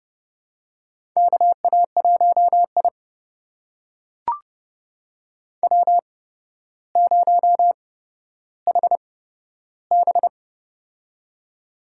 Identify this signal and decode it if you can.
Morse code, heard as a keyed tone, about 30 wpm, spaced as for 13 wpm, 700 Hz. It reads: KA1I W0HB
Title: USA Call Signs - 8x Character Spacing - 30wpm